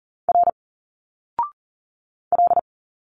Morse code, heard as a keyed tone, about 40 wpm, 700 Hz. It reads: R L
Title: Single Letters 40wpm